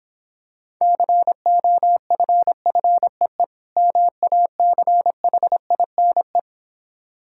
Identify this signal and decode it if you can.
Morse code, heard as a keyed tone, about 26 wpm, 700 Hz. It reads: COFFEE MACHINE